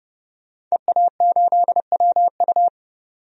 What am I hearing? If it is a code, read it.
Morse code, heard as a keyed tone, about 30 wpm, 700 Hz. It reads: EA8WU
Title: DXCC-100 Call Signs - 5 Characters - 30wpm